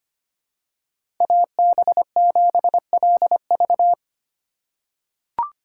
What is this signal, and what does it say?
Morse code, heard as a keyed tone, about 25 wpm, 700 Hz. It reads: AB7LV
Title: USA Call Signs - 4x Character Spacing - 25wpm